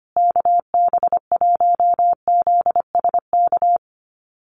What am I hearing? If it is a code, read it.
Morse code, heard as a keyed tone, about 25 wpm, 700 Hz. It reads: KB1ZSK
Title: USA Call Signs Repeated 2x 25wpm